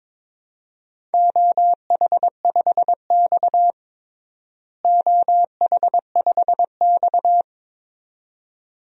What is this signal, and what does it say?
Morse code, heard as a keyed tone, about 22 wpm, 700 Hz. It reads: OH5X OH5X
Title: DXCC-100 Call Signs - 4 Characters - Repeated 2x - 22wpm